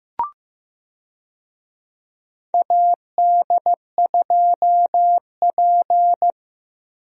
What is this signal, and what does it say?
Morse code, heard as a keyed tone, about 15 wpm, 700 Hz. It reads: AD2P
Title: USA Call Signs - 4 Characters - Repeated 3x - 15wpm